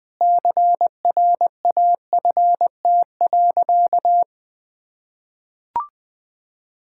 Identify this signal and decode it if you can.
Morse code, heard as a keyed tone, about 20 wpm, 700 Hz. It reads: CRAFT.
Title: Sentences from Top 1000 Words - Review Part 6 - 20wpm